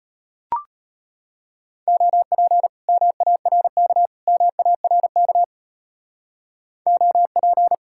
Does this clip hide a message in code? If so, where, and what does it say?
Yes: Morse code from 1.9 to 7.7 s, heard as a keyed tone, about 32 wpm, 700 Hz. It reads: OP MARK MARK OP